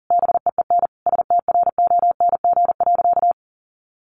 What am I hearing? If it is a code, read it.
Morse code, heard as a keyed tone, about 40 wpm, 700 Hz. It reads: BEEN STRONG.
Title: Sentences from Top 1400 Words 40wpm